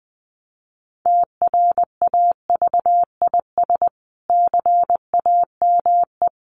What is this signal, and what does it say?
Morse code, heard as a keyed tone, about 20 wpm, 700 Hz. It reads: TRAVIS CAME
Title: A Sound Of Thunder 20wpm